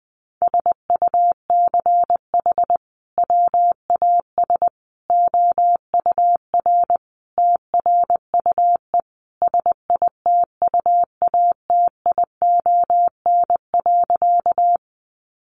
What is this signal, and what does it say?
Morse code, heard as a keyed tone, about 20 wpm, 700 Hz. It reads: SUCH WAS OUR TRUE SITUATION.
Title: Sentences from Top 1200 Words 20wpm